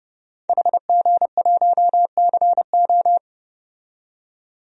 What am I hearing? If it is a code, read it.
Morse code, heard as a keyed tone, about 30 wpm, 700 Hz. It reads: HG1CO
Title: DXCC-100 Call Signs - 5 Characters - 30wpm